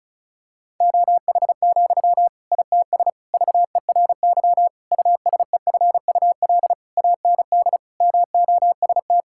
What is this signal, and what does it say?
Morse code, heard as a keyed tone, about 34 wpm, 700 Hz. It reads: OH, ITS VERY USEFUL AND MOST
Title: Sentences from Top 1000 Words - Review Part 6 - 35wpm